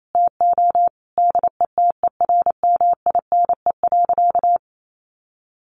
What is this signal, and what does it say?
Morse code, heard as a keyed tone, about 28 wpm, 700 Hz. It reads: TO DETERMINE.
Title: Sentences from Top 1500 Words 28wpm